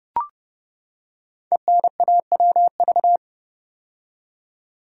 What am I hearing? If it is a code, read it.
Morse code, heard as a keyed tone, about 30 wpm, 690 Hz. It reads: ENAWV
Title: Code Groups - 5 Letters - 30wpm